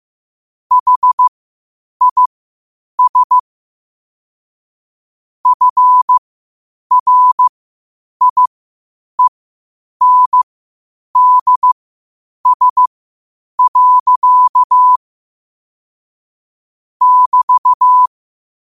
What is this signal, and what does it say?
Morse code, heard as a keyed tone, about 15 wpm, spaced as for 8 wpm, 1000 Hz. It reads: HIS FRIENDS. <BT>
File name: qotdfiles / 8wpm